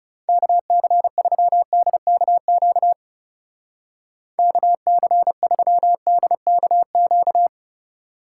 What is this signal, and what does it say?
Morse code, heard as a keyed tone, about 38 wpm, 700 Hz. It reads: KC3DKQ KC3DKQ